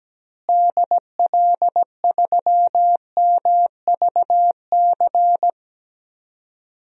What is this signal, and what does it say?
Morse code, heard as a keyed tone, about 17 wpm, 700 Hz. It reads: DL3MVC
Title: DXCC-100 Call Signs - 6 Characters - Repeated 3x - 17wpm